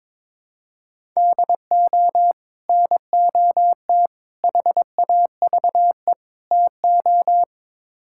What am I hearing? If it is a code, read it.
Morse code, heard as a keyed tone, about 22 wpm, 700 Hz. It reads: DO NOT HAVE TO
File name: practice / 22wpm / Build A Phrase - 5-Word Phrases - 22wpm